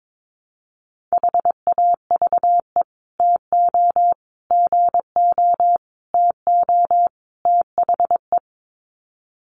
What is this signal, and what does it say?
Morse code, heard as a keyed tone, about 22 wpm, 700 Hz. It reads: HAVE TO GO TO THE
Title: Sets of 5 Words Encore 22wpm